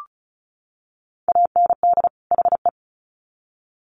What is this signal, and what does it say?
Morse code, heard as a keyed tone, about 34 wpm, 700 Hz. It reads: AND HE
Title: Sets of 2 Words 35wpm